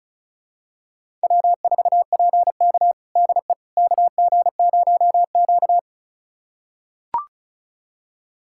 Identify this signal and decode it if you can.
Morse code, heard as a keyed tone, about 36 wpm, 700 Hz. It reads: W4PK DE KG0Q
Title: USA Callsign DE Callsign - 4 Characters - 35wpm